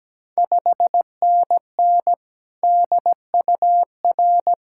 Morse code, heard as a keyed tone, about 17 wpm, 700 Hz. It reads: 5NN DUR
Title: Ontario QSO Party Exchange 17wpm